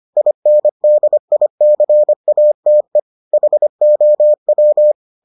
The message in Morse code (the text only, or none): INDICATE HOW